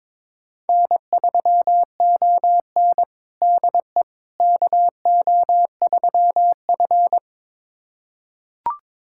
Morse code, heard as a keyed tone, about 22 wpm, 700 Hz. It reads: N3ON DE KO3F
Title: USA Callsign DE Callsign - 4 Characters - 22wpm